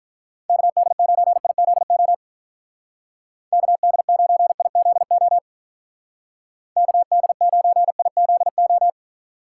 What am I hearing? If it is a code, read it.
Morse code, heard as a keyed tone, about 46 wpm, 700 Hz. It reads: KD9IZO KD9IZO KD9IZO